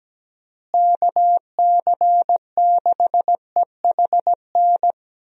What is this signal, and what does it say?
Morse code, heard as a keyed tone, about 17 wpm, 700 Hz. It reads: KC6EHN
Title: USA Call Signs - 6 Characters - Repeated 3x - 17wpm